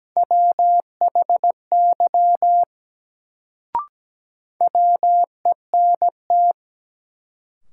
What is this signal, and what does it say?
Morse code, heard as a keyed tone, about 17 wpm, 700 Hz. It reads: WHY WENT
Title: Sentences from Top 200 Words - New Word Drill - 17wpm